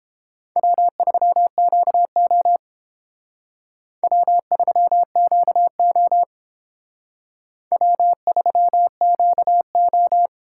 Morse code, heard as a keyed tone, about 30 wpm, 700 Hz. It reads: W3QO W3QO W3QO